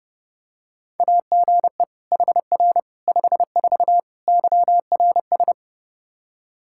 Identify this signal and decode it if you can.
Morse code, heard as a keyed tone, about 29 wpm, 700 Hz. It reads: AGE HR 54 YRS
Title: Practice QSO - Exchange 3 - More Info - 30wpm